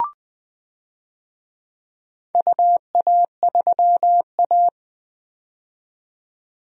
Morse code, heard as a keyed tone, about 20 wpm, 700 Hz. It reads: UA3A
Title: DXCC-100 Call Signs - 4 Characters - 20wpm